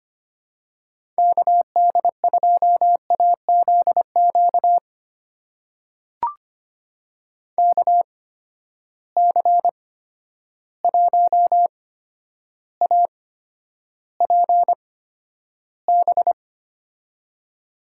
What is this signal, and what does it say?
Morse code, heard as a keyed tone, about 25 wpm, spaced as for 11 wpm, 700 Hz. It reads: KD2AZQ KC1APB
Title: USA Call Signs - 6 Characters - 8x Character Spacing - 25wpm